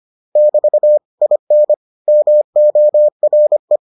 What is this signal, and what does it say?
Morse code, heard as a keyed tone, about 25 wpm, 600 Hz. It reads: <BT> IN MORE